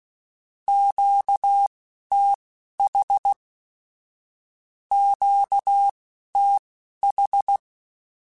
Morse code, heard as a keyed tone, about 16 wpm, spaced as for 11 wpm, 800 Hz. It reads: QTH QTH
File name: (spaced) / Q-Calls Spaced-mp3s / Q Calls Twice sp @15